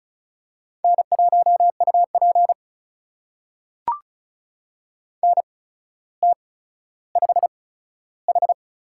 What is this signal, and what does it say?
Morse code, heard as a keyed tone, about 36 wpm, spaced as for 15 wpm, 700 Hz. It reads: N1UP NT5H
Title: USA Call Signs - 8x Character Spacing - 35wpm